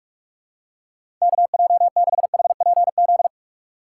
Morse code, heard as a keyed tone, about 44 wpm, 690 Hz. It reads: KJ6HPZ
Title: USA Call Signs - 4x Character Spacing - 45wpm